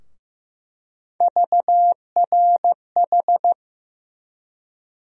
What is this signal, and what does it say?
Morse code, heard as a keyed tone, about 15 wpm, 700 Hz. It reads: VRH
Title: USA Call Sign Postfixes - 3 Letters 15wpm